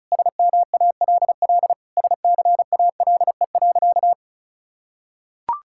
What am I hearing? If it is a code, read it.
Morse code, heard as a keyed tone, about 36 wpm, 700 Hz. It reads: SMALL SCALE.
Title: Sentences from Top 1000 Words - Review Part 5 - 35wpm